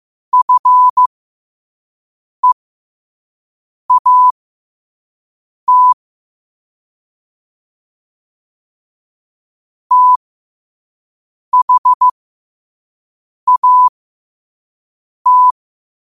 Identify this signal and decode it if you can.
Morse code, heard as a keyed tone, about 15 wpm, spaced as for 5 wpm, 1000 Hz. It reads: FEAT THAT
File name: qotdfiles / 5wpm